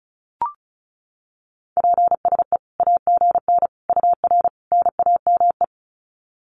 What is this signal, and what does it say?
Morse code, heard as a keyed tone, about 34 wpm, 700 Hz. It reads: PSE AGN UR NAME
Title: Common QSO Phrases 35wpm